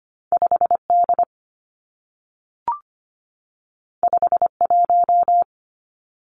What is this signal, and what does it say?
Morse code, heard as a keyed tone, about 25 wpm, 700 Hz. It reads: 5D 51
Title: Two Challenging Letter-Number Combinations 25wpm